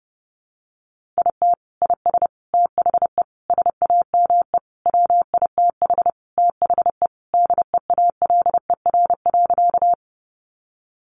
Morse code, heard as a keyed tone, about 30 wpm, 700 Hz. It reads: IT IS THE SAME WITH THE DEALER.
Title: Sentences from Top 1100 Words 30wpm